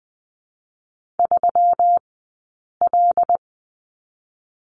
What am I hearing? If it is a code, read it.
Morse code, heard as a keyed tone, about 20 wpm, 700 Hz. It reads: 3 L
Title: 120 Lesson 20d - Number 3 - Two Characters 20wpm